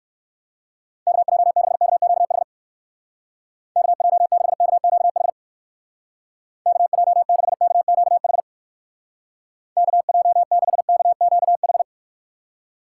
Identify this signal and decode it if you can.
Morse code, heard as a keyed tone, about 57 wpm, 700 Hz. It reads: KJ6KQH KJ6KQH KJ6KQH KJ6KQH